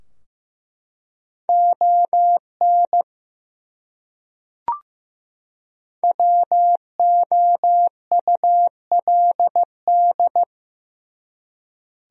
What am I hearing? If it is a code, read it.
Morse code, heard as a keyed tone, about 15 wpm, 700 Hz. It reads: ON WOULD
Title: Top 100 Words 15wpm